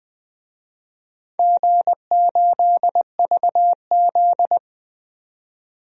Morse code, heard as a keyed tone, about 20 wpm, 700 Hz. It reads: G8VZ
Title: DXCC-100 Call Signs - 4 Characters - 20wpm